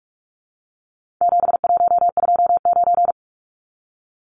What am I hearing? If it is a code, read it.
Morse code, heard as a keyed tone, about 44 wpm, 700 Hz. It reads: 7129
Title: Numbers - 4-Digit 45wpm